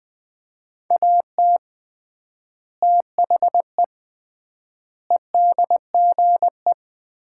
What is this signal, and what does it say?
Morse code, heard as a keyed tone, about 20 wpm, 700 Hz. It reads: AT THE EDGE
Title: 3x Word-Spacing - Sets of 4 Words - 20wpm